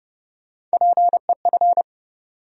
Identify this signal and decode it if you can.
Morse code, heard as a keyed tone, about 30 wpm, 700 Hz. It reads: PEF
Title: USA Call Sign Postfixes - 3 Letters 30wpm